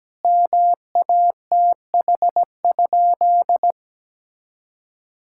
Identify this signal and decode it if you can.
Morse code, heard as a keyed tone, about 17 wpm, 700 Hz. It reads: MATH?
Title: Sentences from Top 1000 Words - Review Part 6 - 17wpm